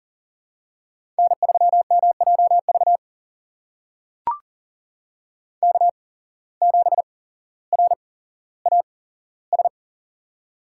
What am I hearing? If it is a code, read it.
Morse code, heard as a keyed tone, about 38 wpm, spaced as for 17 wpm, 700 Hz. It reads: N3MJV K7RAS
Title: USA Call Signs - 5 Characters - 8x Character Spacing - 40wpm